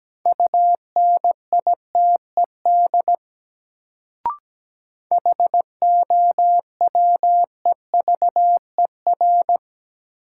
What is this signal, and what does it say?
Morse code, heard as a keyed tone, about 17 wpm, 700 Hz. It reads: UNITED HOWEVER
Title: Sentences from Top 600 Words - New Word Drill - 17wpm